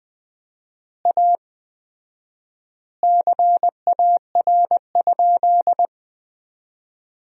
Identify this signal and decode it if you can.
Morse code, heard as a keyed tone, about 20 wpm, 700 Hz. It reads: A CAR?
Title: 4x Word-Spacing - Sentences from Top 300 Words - 20wpm